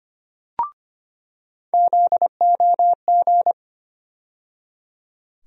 Morse code, heard as a keyed tone, about 25 wpm, 700 Hz. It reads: ZOG
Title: USA Call Sign Postfixes - 3 Letters 25wpm